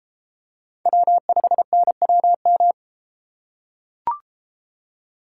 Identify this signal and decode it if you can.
Morse code, heard as a keyed tone, about 32 wpm, 700 Hz. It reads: W5NWM